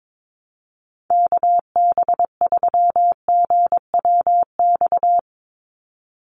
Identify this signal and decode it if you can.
Morse code, heard as a keyed tone, about 22 wpm, 700 Hz. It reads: KB3GWX
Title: USA Call Signs 22wpm